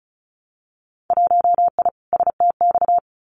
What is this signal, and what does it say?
Morse code, heard as a keyed tone, about 34 wpm, 700 Hz. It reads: 1I STX
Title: Winter Field Day Exchanges 35wpm